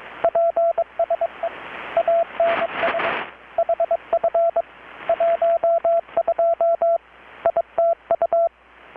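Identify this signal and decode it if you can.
Morse code, heard as a keyed tone, about 22 wpm, 670 Hz, with radio noise. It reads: PSE ANS HF 12 ITU
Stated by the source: Morse from KPH's Night of Nights XX maritime radiotelegraphy event, received in Chiapas, Mexico about 3,800 km away on a Kenwood R-600 with a grounded 10 m long-wire antenna.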